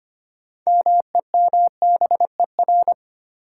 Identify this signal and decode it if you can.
Morse code, heard as a keyed tone, about 25 wpm, 700 Hz. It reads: MEMBER
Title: Sets of 3 Words - New Word Drill - 25wpm